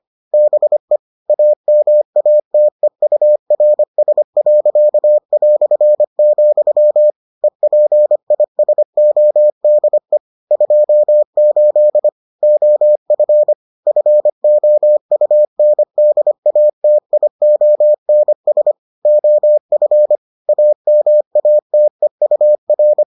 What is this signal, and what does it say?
Morse code, heard as a keyed tone, about 25 wpm, 600 Hz. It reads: BE AMATEURS.", EPISODE 28 OF FOUNDATIONS OF AMATEUR